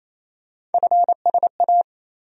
Morse code, heard as a keyed tone, about 28 wpm, 700 Hz. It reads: FSA
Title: USA Call Sign Postfixes - 3 Letters 28wpm